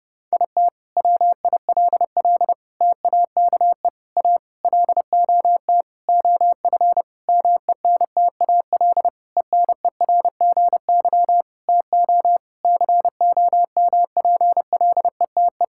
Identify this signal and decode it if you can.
Morse code, heard as a keyed tone, about 30 wpm, 700 Hz. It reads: IT WILL TAKE A LOT OF MENTAL ENERGY TO COMPLETE
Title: Sentences from Top 1000 Words - Review Part 6 - 30wpm